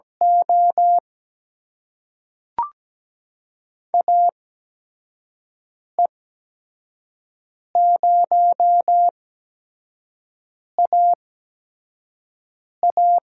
Morse code, heard as a keyed tone, about 17 wpm, spaced as for 5 wpm, 700 Hz. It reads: O AE0AA
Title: USA Call Signs - 8x Character Spacing - 17wpm